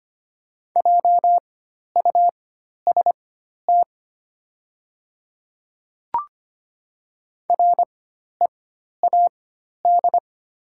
Morse code, heard as a keyed tone, about 25 wpm, spaced as for 12 wpm, 700 Hz. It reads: JUST READ
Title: 4x Character Spacing - Top 200 Words - 25wpm